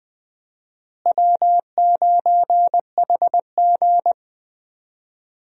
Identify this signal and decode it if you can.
Morse code, heard as a keyed tone, about 20 wpm, 700 Hz. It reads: W9HG
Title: USA Call Signs - 4 Characters - Repeated 3x - 20wpm